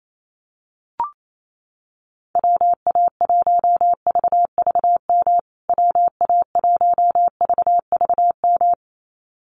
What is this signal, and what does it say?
Morse code, heard as a keyed tone, about 28 wpm, 700 Hz. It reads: WA1VVM WA1VVM
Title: USA Call Signs Repeated 2x 28wpm